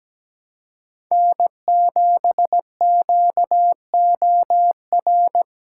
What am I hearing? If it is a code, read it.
Morse code, heard as a keyed tone, about 17 wpm, 700 Hz. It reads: N7QOR
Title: USA Callsign DE Callsign 17wpm